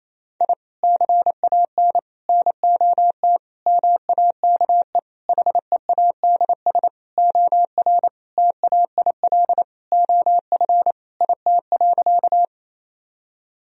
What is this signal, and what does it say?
Morse code, heard as a keyed tone, about 28 wpm, 700 Hz. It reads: I CAN NOT MAKE HEADS OR TAIL OF IT.
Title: Sentences from Top 1000 Words - Review Part 3 - 28wpm